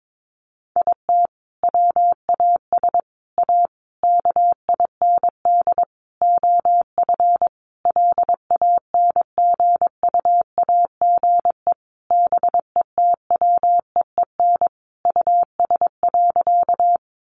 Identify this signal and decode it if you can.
Morse code, heard as a keyed tone, about 22 wpm, 700 Hz. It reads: IT WAS A KIND OF LANGUAGE BETWEEN US.
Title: Sentences from Top 500 Words 22wpm